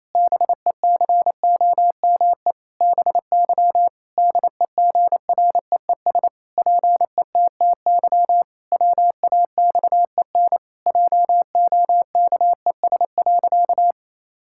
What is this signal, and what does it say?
Morse code, heard as a keyed tone, about 28 wpm, 700 Hz. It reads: BECOME BY DEGREES PETTY WAXEN JOKES.
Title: Pangram Challenge v2 - 28wpm